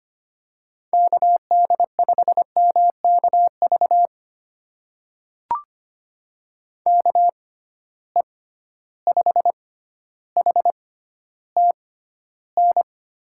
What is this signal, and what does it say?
Morse code, heard as a keyed tone, about 25 wpm, spaced as for 13 wpm, 700 Hz. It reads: KD5MKV KE5HTN
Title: USA Call Signs - 6 Characters - 6x Character Spacing - 25wpm